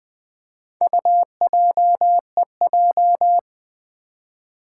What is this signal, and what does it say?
Morse code, heard as a keyed tone, about 20 wpm, 700 Hz. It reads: UJEJ
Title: Four Challenging Letter-Number Combinations 20wpm